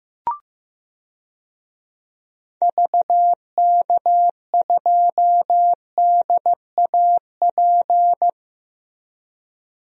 Morse code, heard as a keyed tone, about 15 wpm, 700 Hz. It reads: VK2DAP
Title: DXCC-100 Call Signs - 6 Characters - Repeated 3x - 15wpm